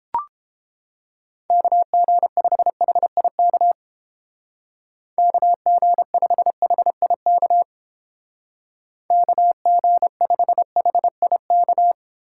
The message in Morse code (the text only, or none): KG5HIK KG5HIK KG5HIK